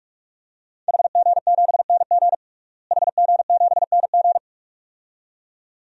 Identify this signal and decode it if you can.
Morse code, heard as a keyed tone, about 46 wpm, 700 Hz. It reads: HG7NG HG7NG
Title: DXCC-100 Call Signs - 5 Characters - Repeated 2x - 45wpm